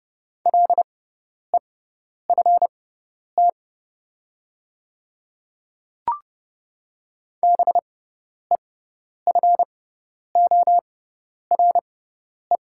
Morse code, heard as a keyed tone, about 30 wpm, spaced as for 10 wpm, 700 Hz. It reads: LEFT BEFORE
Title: 6x Character Spacing - Top 300 Words - 30wpm